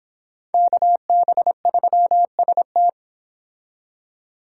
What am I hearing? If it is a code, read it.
Morse code, heard as a keyed tone, about 26 wpm, 700 Hz. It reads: KB3ST